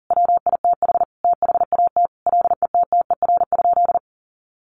Morse code, heard as a keyed tone, about 40 wpm, 700 Hz. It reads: WITH THAT LETTER?
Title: Sentences from Top 300 Words 40wpm